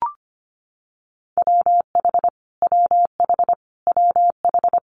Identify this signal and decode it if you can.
Morse code, heard as a keyed tone, about 25 wpm, 700 Hz. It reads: WH WH WH
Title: Multi-letter Phonemes Repeated 3x 25wpm